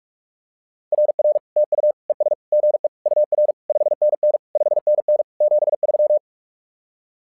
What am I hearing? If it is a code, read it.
Morse code, heard as a keyed tone, about 46 wpm, 600 Hz. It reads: RR TU ES GE UR 5NN 5NN 73
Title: SOTA QSO Practice 45wpm